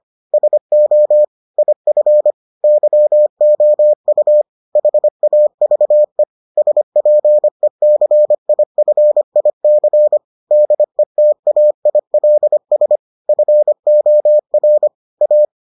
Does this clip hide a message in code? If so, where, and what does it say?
Yes: Morse code from 0.3 to 15.4 s, heard as a keyed tone, about 25 wpm, 600 Hz. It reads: SO IF YOU HAVE SPECIFIC DETAILS FOR A